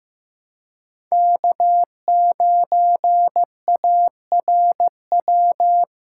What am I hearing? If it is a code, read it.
Morse code, heard as a keyed tone, about 15 wpm, 700 Hz. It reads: K9ARW
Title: USA Call Signs - 5 Characters - 3x Character Spacing - 15wpm